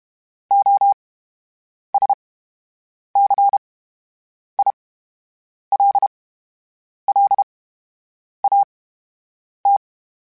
Morse code, heard as a keyed tone, about 32 wpm, spaced as for 8 wpm, 800 Hz. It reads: OSCILLAT
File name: Special INGs / Special INGs-mp3s / VerySlowToFaster Ing @30